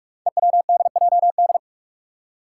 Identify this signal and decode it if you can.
Morse code, heard as a keyed tone, about 44 wpm, 700 Hz. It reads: EWDJD
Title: Five Challenging Letter-Number Combinations 45wpm